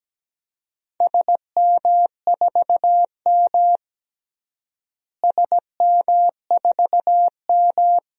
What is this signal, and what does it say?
Morse code, heard as a keyed tone, about 17 wpm, 700 Hz. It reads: SM4M SM4M
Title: DXCC-100 Call Signs - 4 Characters - Repeated 2x - 17wpm